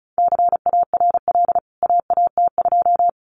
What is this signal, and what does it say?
Morse code, heard as a keyed tone, about 34 wpm, 700 Hz. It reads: CARL AAT2
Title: CWT Exchange 35wpm